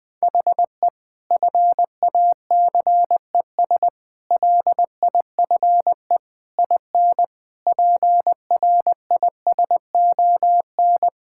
Morse code, heard as a keyed tone, about 20 wpm, 700 Hz. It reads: HE FACES LIFE IN PRISON
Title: Sentences with 8-Word Phrases - Spoken Beginning - 20wpm